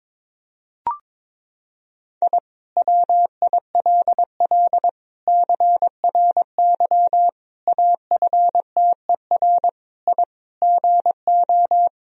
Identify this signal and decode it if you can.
Morse code, heard as a keyed tone, about 22 wpm, 700 Hz. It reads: I WILL CRY AFTER I GO
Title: Sentences from Top 1000 Words - Review Part 2 - 22wpm